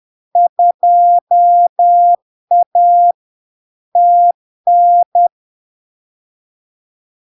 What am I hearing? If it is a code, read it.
Morse code, heard as a keyed tone, about 10 wpm, 700 Hz. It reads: 2A TN
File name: LARK-FD-2022-Log-in-CW-10wpm-700-Hz